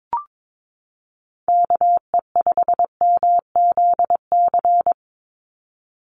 Morse code, heard as a keyed tone, about 22 wpm, 700 Hz. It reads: KE5MZC
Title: USA Call Signs 22wpm